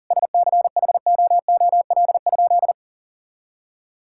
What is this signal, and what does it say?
Morse code, heard as a keyed tone, about 37 wpm, 690 Hz. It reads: SCHOOL?
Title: Sentences from Top 1000 Words - Review Part 6 - 40wpm